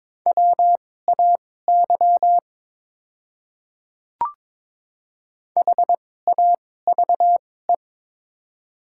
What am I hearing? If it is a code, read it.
Morse code, heard as a keyed tone, about 22 wpm, spaced as for 16 wpm, 700 Hz. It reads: WAY HAVE
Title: 2x Character Spacing - Top 200 Words - 22wpm